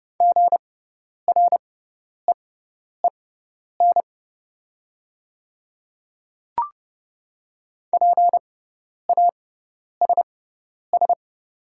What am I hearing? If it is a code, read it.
Morse code, heard as a keyed tone, about 29 wpm, spaced as for 10 wpm, 700 Hz. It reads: GREEN PASS
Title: 6x Character Spacing - Top 500 Words - 30wpm